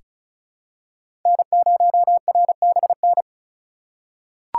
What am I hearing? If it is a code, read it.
Morse code, heard as a keyed tone, about 35 wpm, 700 Hz. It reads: N0RBN
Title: USA Call Signs - 5 Characters - 2x Character Spacing - 35wpm